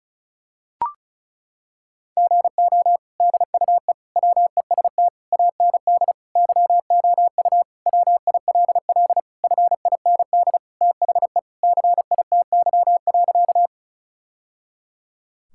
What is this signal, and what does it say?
Morse code, heard as a keyed tone, about 34 wpm, 700 Hz. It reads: GO DUE WEST AND YOU WILL FIND THE CITY.
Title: Sentences from Top 1000 Words - Review Part 6 - 35wpm